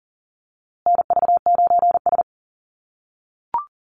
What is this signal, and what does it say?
Morse code, heard as a keyed tone, about 40 wpm, 700 Hz. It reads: NV9S
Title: USA Call Signs Repeated 3x 40wpm